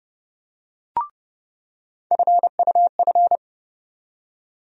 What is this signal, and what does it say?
Morse code, heard as a keyed tone, about 30 wpm, 700 Hz. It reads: FUF
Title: USA Call Sign Postfixes - 3 Letters 30wpm